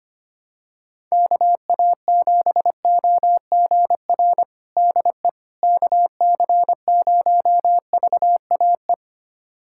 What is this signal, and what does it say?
Morse code, heard as a keyed tone, about 25 wpm, 700 Hz. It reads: KA7OGR DE KC0VAE
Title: USA Callsign DE Callsign - 6 Characters - 25wpm